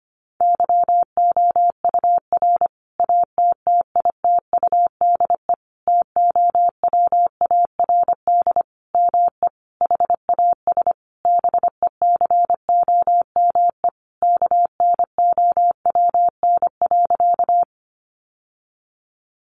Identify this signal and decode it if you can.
Morse code, heard as a keyed tone, about 25 wpm, 700 Hz. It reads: YOUR ATTITUDE TOWARD ME HAS BECOME KNOWN.
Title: Sentences from Top 1300 Words 25wpm